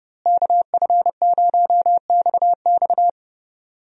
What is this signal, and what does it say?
Morse code, heard as a keyed tone, about 30 wpm, 700 Hz. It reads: KF0XX
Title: USA Call Signs - 5 Characters - Repeated 2x - 30wpm